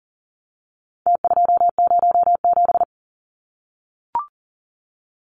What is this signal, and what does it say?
Morse code, heard as a keyed tone, about 40 wpm, 700 Hz. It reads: T207
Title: Numbers - Contest Style - 4-Digit 40wpm